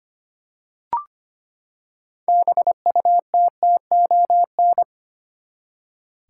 Morse code, heard as a keyed tone, about 25 wpm, 700 Hz. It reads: BUTTON
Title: Six Letter Words - 25wpm